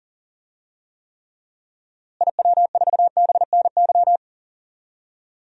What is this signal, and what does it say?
Morse code, heard as a keyed tone, about 40 wpm, 700 Hz. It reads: IW4BNY
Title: DXCC-100 Call Signs - 6 Characters - Repeated 3x - 40wpm